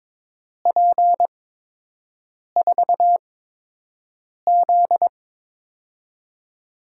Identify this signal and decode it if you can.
Morse code, heard as a keyed tone, about 22 wpm, spaced as for 6 wpm, 700 Hz. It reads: P4Z
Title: USA Call Signs - 8x Character Spacing - 22wpm